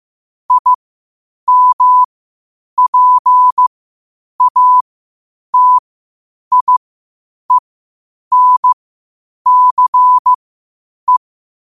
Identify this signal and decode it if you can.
Morse code, heard as a keyed tone, about 15 wpm, spaced as for 8 wpm, 1000 Hz. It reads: IMPATIENCE
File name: qotdfiles / 8wpm